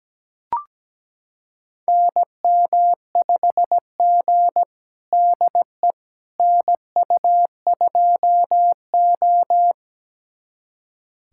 Morse code, heard as a keyed tone, about 17 wpm, 700 Hz. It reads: NM5G DE NU2O
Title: USA Callsign DE Callsign - 4 Characters - 17wpm